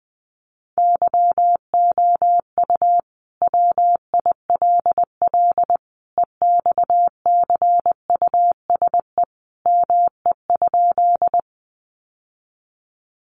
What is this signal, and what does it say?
Morse code, heard as a keyed tone, about 20 wpm, 700 Hz. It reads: YOU WILL EXCUSE ME?
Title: Sentences from Top 700 Words 20wpm